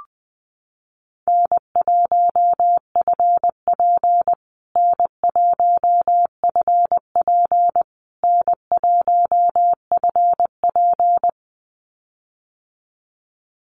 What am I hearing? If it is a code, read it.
Morse code, heard as a keyed tone, about 20 wpm, 700 Hz. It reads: N1FP N1FP N1FP
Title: USA Call Signs Repeated 3x 20wpm